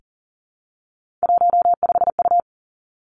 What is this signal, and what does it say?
Morse code, heard as a keyed tone, about 41 wpm, 700 Hz. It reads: 15U
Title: Three Challenging Letter-Number Combinations 40wpm